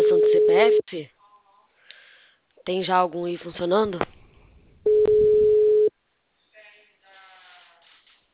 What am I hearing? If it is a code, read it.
Ringback tone (Europe, 425 Hz), with speech.